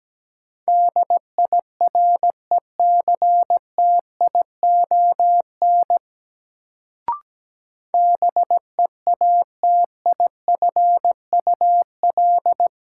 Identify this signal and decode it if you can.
Morse code, heard as a keyed tone, about 17 wpm, 700 Hz. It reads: DIRECTION BEATIFUL
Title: Sentences from Top 1300 Words - New Word Drill Alt - 17wpm